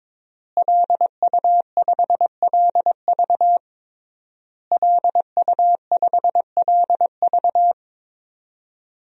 Morse code, heard as a keyed tone, about 22 wpm, 700 Hz. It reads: LU5LV LU5LV
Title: DXCC-100 Call Signs - 5 Characters - Repeated 3x - 22wpm